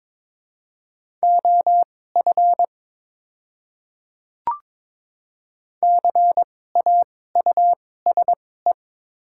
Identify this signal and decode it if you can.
Morse code, heard as a keyed tone, about 22 wpm, spaced as for 16 wpm, 700 Hz. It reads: OF CAUSE
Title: 2x Character Spacing - Top 200 Words - 22wpm